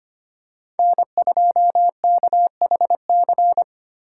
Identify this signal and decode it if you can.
Morse code, heard as a keyed tone, about 25 wpm, 700 Hz. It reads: N2KHC
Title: USA Call Signs - 2x Character Spacing - 25wpm